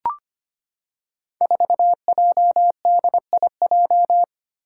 Morse code, heard as a keyed tone, about 25 wpm, 700 Hz. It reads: 4JDIJ
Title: Five Challenging Letter-Number Combinations 25wpm